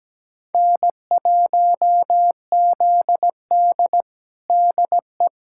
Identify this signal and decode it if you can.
Morse code, heard as a keyed tone, about 17 wpm, 700 Hz. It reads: N1ZD DE
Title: USA Callsign DE Callsign - 4 Characters - 17wpm